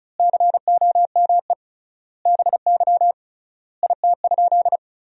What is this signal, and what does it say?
Morse code, heard as a keyed tone, about 36 wpm, 700 Hz. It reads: COME BY IT?
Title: 3x Word-Spacing - Sentences from Top 100 Words - 35wpm